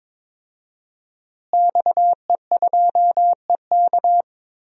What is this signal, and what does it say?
Morse code, heard as a keyed tone, about 22 wpm, 700 Hz. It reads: XE2EK
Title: DXCC-100 Call Signs - 5 Characters - Repeated 3x - 22wpm